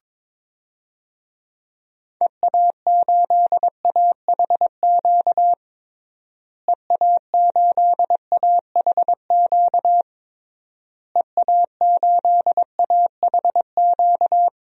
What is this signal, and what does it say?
Morse code, heard as a keyed tone, about 22 wpm, 700 Hz. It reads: EA8AHQ EA8AHQ EA8AHQ
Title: DXCC-100 Call Signs - 6 Characters - Repeated 3x - 22wpm